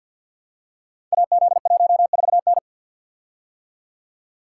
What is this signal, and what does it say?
Morse code, heard as a keyed tone, about 51 wpm, 700 Hz. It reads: AZ14N
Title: Code Groups - 5 Letter:Numbers - 50wpm